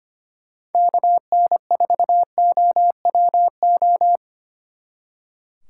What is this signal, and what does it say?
Morse code, heard as a keyed tone, about 30 wpm, 700 Hz. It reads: KN4OWO
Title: Speed Racing - USA 2x3 Call Signs 25wpm